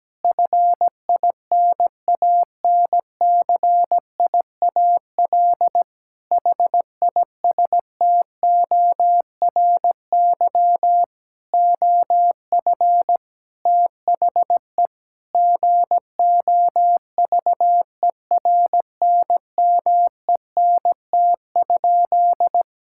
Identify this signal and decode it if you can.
Morse code, heard as a keyed tone, about 17 wpm, 700 Hz. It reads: FINANCIAL HISTORY OF THE GOVERNMENT?
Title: Sentences from Top 1000 Words - Review Part 5 - 17wpm